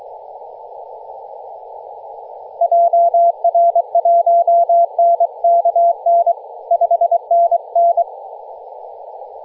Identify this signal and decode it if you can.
Morse code, heard as a keyed tone, about 23 wpm, 680 Hz, with radio noise. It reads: JR1NKN 5NN
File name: posts / 2016-nov-13-0348z-t30jy-28-034mhz-cw